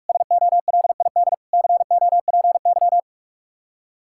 Morse code, heard as a keyed tone, about 59 wpm, spaced as for 52 wpm, 700 Hz. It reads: SOLID COPY
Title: Speed-Racing Common QSO Phrases Encore - 45wpm